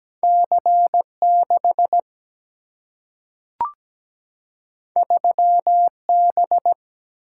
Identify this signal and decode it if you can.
Morse code, heard as a keyed tone, about 17 wpm, 700 Hz. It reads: C 6 3 B
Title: Single-Letter Single-Number 17wpm